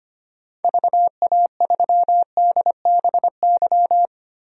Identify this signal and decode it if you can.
Morse code, heard as a keyed tone, about 25 wpm, 700 Hz. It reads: VA3DBY
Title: DXCC-100 Call Signs - 6 Characters - 25wpm